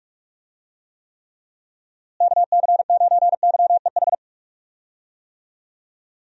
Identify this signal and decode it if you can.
Morse code, heard as a keyed tone, about 44 wpm, 700 Hz. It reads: KC9YEH
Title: USA Call Signs - 6 Characters - Repeated 2x - 45wpm